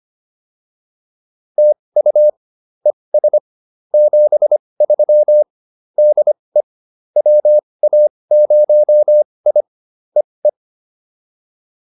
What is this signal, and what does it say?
Morse code, heard as a keyed tone, about 25 wpm, spaced as for 20 wpm, 600 Hz. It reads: TU ES 73 DE WA0I EE
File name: POTA107 20